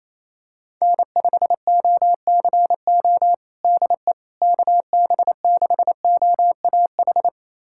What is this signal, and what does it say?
Morse code, heard as a keyed tone, about 28 wpm, 700 Hz. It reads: N5OCO DE KB6OAH
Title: USA Callsign DE Callsign 28wpm